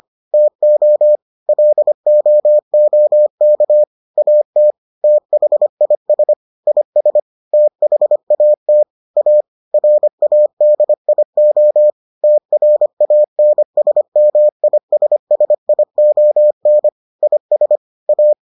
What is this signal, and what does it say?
Morse code, heard as a keyed tone, about 25 wpm, 600 Hz. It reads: TO LOOK AT THIS IS THAT A RADIO TRANSMISSION IS A